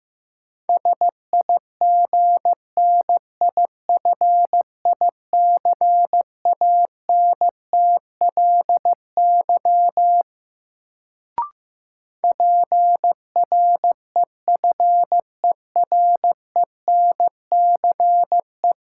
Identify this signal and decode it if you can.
Morse code, heard as a keyed tone, about 15 wpm, 700 Hz. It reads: SIGNIFICANTLY PREFERENCE
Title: Sentences from Top 1500 Words - New Word Drill - 15wpm